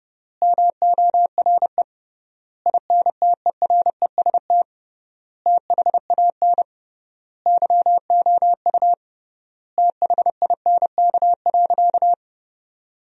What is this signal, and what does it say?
Morse code, heard as a keyed tone, about 31 wpm, 700 Hz. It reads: MORE INTEREST THAN YOU THINK.
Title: 3x Word-Spacing - Sentences from Top 400 Words - 30wpm